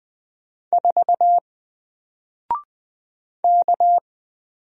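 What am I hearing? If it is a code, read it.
Morse code, heard as a keyed tone, about 20 wpm, 700 Hz. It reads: 4 K
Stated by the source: Single letter and number Morse drill at 20 wpm, from N4LCD.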